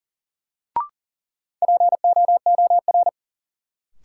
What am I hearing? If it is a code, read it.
Morse code, heard as a keyed tone, about 40 wpm, 700 Hz. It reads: POOR
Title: Sentences from Top 1000 Words - New Word Drill - 40wpm